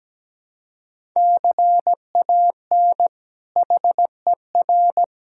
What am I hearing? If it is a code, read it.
Morse code, heard as a keyed tone, about 17 wpm, 700 Hz. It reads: CAN HER
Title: Sentences from Top 1000 Words - Review Part 5 - 17wpm